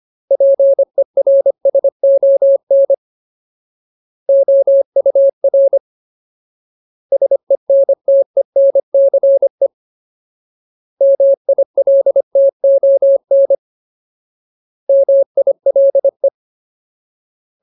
Morse code, heard as a keyed tone, about 25 wpm, 550 Hz. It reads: PERSON OUR SENTENCE MILTON MILE
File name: Ultimate Word List / 25WPM Ultimate